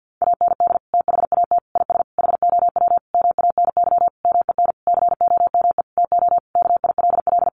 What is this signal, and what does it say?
Morse code, heard as a keyed tone, about 50 wpm, 700 Hz. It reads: AND THAT IS HOW MANY MEN COME TO KILL
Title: Sentences from Top 1400 Words 50wpm